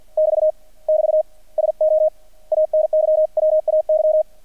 Morse code, heard as a keyed tone, about 50 wpm, 620 Hz, with radio noise. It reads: <BT> <BT> SO ANY WAY
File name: archived / CRSnet